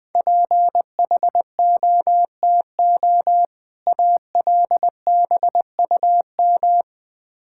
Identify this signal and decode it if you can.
Morse code, heard as a keyed tone, about 20 wpm, 700 Hz. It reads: PHOTO ALBUM
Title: Difficult Words with Hints - 20wpm